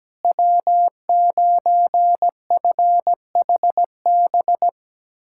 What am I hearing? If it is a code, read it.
Morse code, heard as a keyed tone, about 17 wpm, 700 Hz. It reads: W9FHB
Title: USA Callsign DE Callsign - 5 Characters - 17wpm